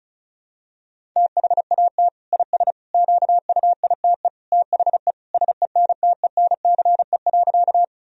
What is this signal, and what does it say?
Morse code, heard as a keyed tone, about 34 wpm, 700 Hz. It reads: THAT IS QUITE THE SENTENCE.
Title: Sentences from Top 1000 Words - Review Part 1 - 35wpm